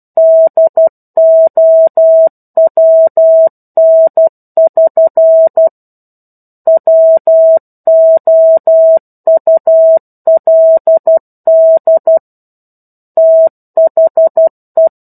Morse code, heard as a keyed tone, about 12 wpm, 650 Hz. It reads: DOWN<SN> WOULD THE
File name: 01-Chapter-01 12 WPM